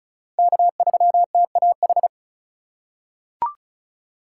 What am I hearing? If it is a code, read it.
Morse code, heard as a keyed tone, about 34 wpm, 700 Hz. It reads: K3TAH
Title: USA Call Signs - 3x Character Spacing - 35wpm